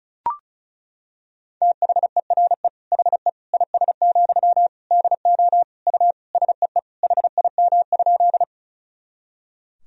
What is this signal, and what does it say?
Morse code, heard as a keyed tone, about 36 wpm, 700 Hz. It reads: THERE HE IS, DO U SEE HIM?
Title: Sentences from Top 100 Words and Abbreviations 35wpm